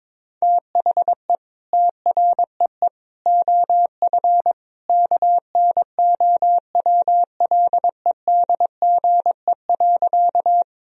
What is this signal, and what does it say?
Morse code, heard as a keyed tone, about 22 wpm, 700 Hz. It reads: THE TREE OF KNOWLEDGE.
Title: Sentences from Top 1000 Words - Review Part 3 - 22wpm